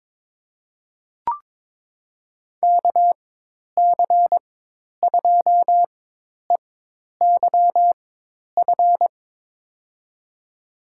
Morse code, heard as a keyed tone, about 23 wpm, spaced as for 10 wpm, 700 Hz. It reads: KC2EYF
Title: USA Call Signs - 6 Characters - 4x Character Spacing - 22wpm